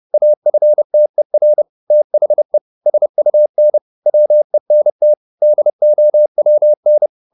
Morse code, heard as a keyed tone, about 30 wpm, 600 Hz. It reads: AFTER THE SUN WENT DOWN
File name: the old man and the sea / the old man and the sea 1 30wpm